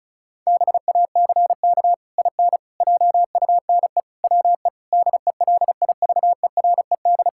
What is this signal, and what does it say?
Morse code, heard as a keyed tone, about 34 wpm, 700 Hz. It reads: BACK IN JUNE WE DELIVERED
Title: Pangram Challenge v2 - 35wpm